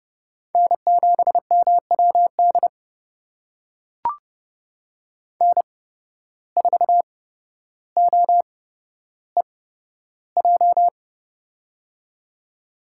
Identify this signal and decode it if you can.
Morse code, heard as a keyed tone, about 30 wpm, spaced as for 13 wpm, 700 Hz. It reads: N7MWD N4OEJ
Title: USA Call Signs - 8x Character Spacing - 30wpm